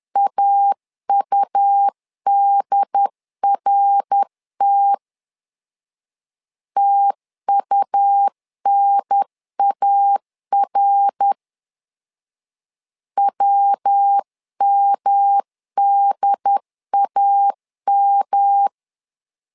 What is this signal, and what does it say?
Morse code, heard as a keyed tone, about 10 wpm, 800 Hz. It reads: AUDRT TUNAR WMDAM
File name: LEZIONI / 3 MWRO / 3 MWR 10wpm 1952